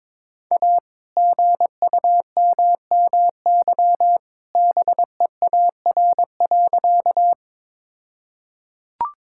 Morse code, heard as a keyed tone, about 22 wpm, 700 Hz. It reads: A GUMMY BEAR.
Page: morse-code-ninja-practice-22wpm